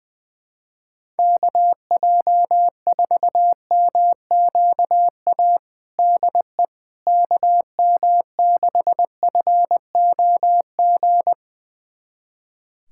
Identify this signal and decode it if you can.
Morse code, heard as a keyed tone, about 21 wpm, 700 Hz. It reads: KJ4MQA DE KM6FOG
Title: USA Callsign DE Callsign - 6 Characters - 20wpm